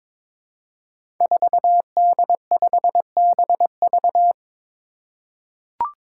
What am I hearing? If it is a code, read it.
Morse code, heard as a keyed tone, about 22 wpm, 700 Hz. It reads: 4D5BV
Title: Five Challenging Letter-Number Combinations 22wpm